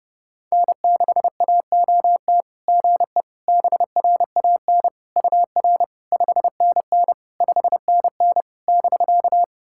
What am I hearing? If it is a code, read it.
Morse code, heard as a keyed tone, about 30 wpm, 660 Hz. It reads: N6AOT GE BRAN UR 5NN 5NN <BK>
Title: SOTA QSO Practice 30wpm